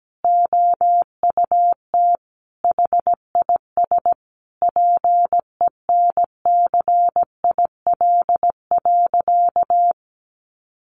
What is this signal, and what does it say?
Morse code, heard as a keyed tone, about 17 wpm, 700 Hz. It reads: OUT HIS PENCIL.
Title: Additional Easy Phrases 17wpm